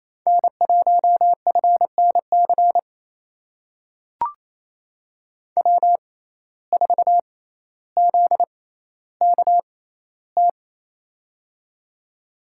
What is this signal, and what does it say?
Morse code, heard as a keyed tone, about 28 wpm, spaced as for 14 wpm, 700 Hz. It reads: N1FNC W4ZKT
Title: USA Call Signs - 5 Characters - 6x Character Spacing - 28wpm